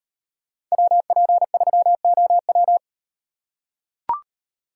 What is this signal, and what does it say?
Morse code, heard as a keyed tone, about 38 wpm, 700 Hz. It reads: WP3OW